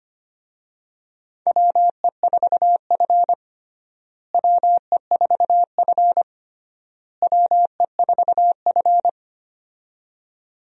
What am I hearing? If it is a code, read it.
Morse code, heard as a keyed tone, about 25 wpm, 700 Hz. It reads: WE4F WE4F WE4F
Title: USA Call Signs - 4 Characters - Repeated 3x - 25wpm